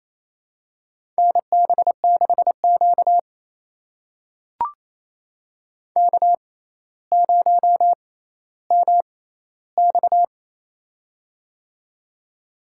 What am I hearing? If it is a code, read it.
Morse code, heard as a keyed tone, about 28 wpm, spaced as for 14 wpm, 700 Hz. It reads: NB6Q K0MX
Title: USA Call Signs - 4 Characters - 6x Character Spacing - 28wpm